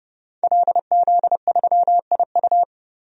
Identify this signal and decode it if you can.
Morse code, heard as a keyed tone, about 30 wpm, 700 Hz. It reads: LZ3IU
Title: DXCC-100 Call Signs - 5 Characters - 30wpm